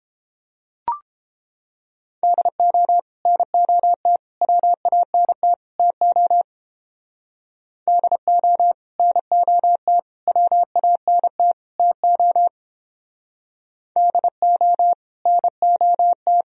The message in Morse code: DO NOT WANT TO DO NOT WANT TO DO NOT

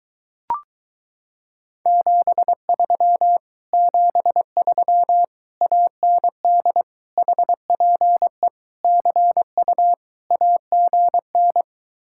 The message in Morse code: 73 73 AND HPE CU AGN

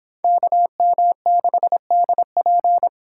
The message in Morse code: KM6DP

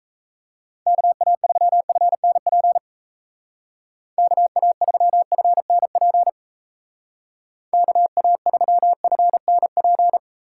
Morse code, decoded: KA3FNP KA3FNP KA3FNP

